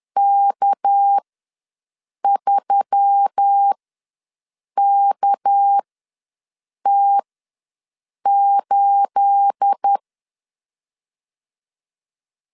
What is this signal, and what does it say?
Morse code, heard as a keyed tone, about 10 wpm, spaced as for 6 wpm, 800 Hz. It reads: K3KT8
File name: LEZIONI / 9 28 ws 10wpm 4124